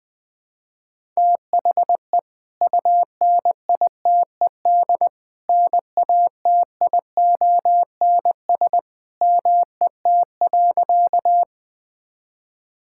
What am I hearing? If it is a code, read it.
Morse code, heard as a keyed tone, about 21 wpm, 700 Hz. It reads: THE UNITED NATIONS MET.
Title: Sentences with 2-Word Phrases - 20wpm